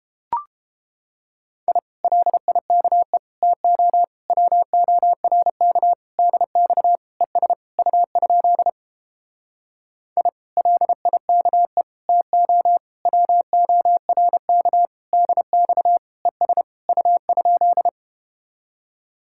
Morse code, heard as a keyed tone, about 31 wpm, 700 Hz. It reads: I LIKE TO WORK DX ES U? I LIKE TO WORK DX ES U?